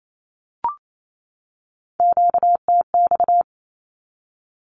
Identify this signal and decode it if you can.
Morse code, heard as a keyed tone, about 28 wpm, 700 Hz. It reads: QTX
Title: Q-Codes 28wpm